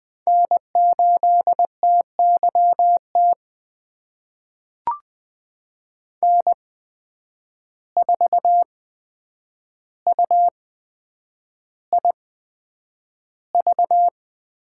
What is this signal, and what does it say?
Morse code, heard as a keyed tone, about 20 wpm, spaced as for 9 wpm, 700 Hz. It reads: N8TYT N4UIV
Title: USA Call Signs - 5 Characters - 8x Character Spacing - 20wpm